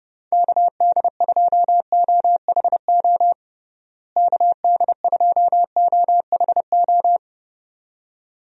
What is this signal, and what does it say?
Morse code, heard as a keyed tone, about 32 wpm, 700 Hz. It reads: KD2OHO KD2OHO
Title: USA Call Signs - 6 Characters - Repeated 2x - 30wpm